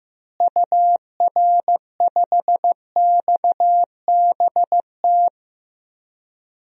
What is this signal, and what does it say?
Morse code, heard as a keyed tone, about 15 wpm, 700 Hz. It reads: UR5XBT
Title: DXCC-100 Call Signs - 6 Characters - Repeated 3x - 15wpm